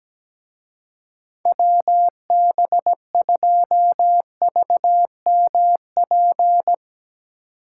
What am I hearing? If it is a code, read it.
Morse code, heard as a keyed tone, about 17 wpm, 700 Hz. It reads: WB2VMP
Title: Speed Racing - USA Call Signs 17wpm